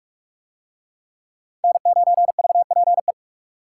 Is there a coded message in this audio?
Yes: Morse code from 1.6 to 3.1 s, heard as a keyed tone, about 38 wpm, 700 Hz. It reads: N9VPE